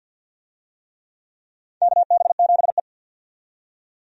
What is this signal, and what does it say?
Morse code, heard as a keyed tone, about 52 wpm, 700 Hz. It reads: KB7E
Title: USA Call Signs - 4 Characters - 50wpm